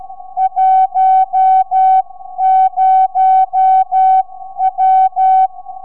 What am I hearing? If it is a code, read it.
Morse code, heard as a keyed tone, about 12 wpm, spaced as for 11 wpm, 760 Hz. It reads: 10W
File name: sv8four 4m